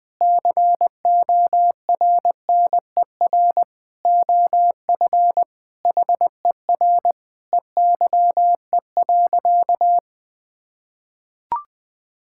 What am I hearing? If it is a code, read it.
Morse code, heard as a keyed tone, about 20 wpm, 700 Hz. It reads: CORNER OF HER EYE.
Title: Sentences with 5-Word Phrases - 20wpm